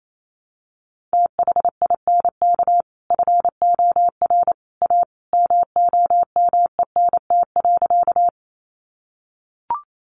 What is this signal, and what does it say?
Morse code, heard as a keyed tone, about 28 wpm, 700 Hz. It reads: THINK FOR A MOMENT.
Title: Sentences from Top 1000 Words 28wpm